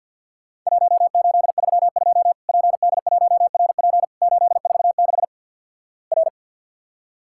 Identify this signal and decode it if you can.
Morse code, heard as a keyed tone, about 50 wpm, 700 Hz. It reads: 1832 PD1RP 846 R
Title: Worked All Europe DX Contest - QTC Exchange - 50wpm